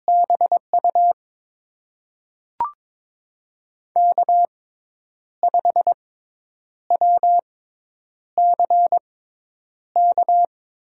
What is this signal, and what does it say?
Morse code, heard as a keyed tone, about 23 wpm, spaced as for 9 wpm, 700 Hz. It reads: BU K5WCK
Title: USA Call Signs - 5 Characters - 6x Character Spacing - 22wpm